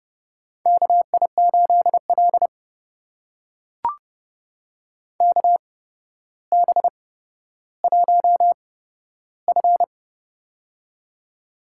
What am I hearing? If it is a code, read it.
Morse code, heard as a keyed tone, about 30 wpm, spaced as for 13 wpm, 700 Hz. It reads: KI8L KB1F
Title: USA Call Signs - 4 Characters - 8x Character Spacing - 30wpm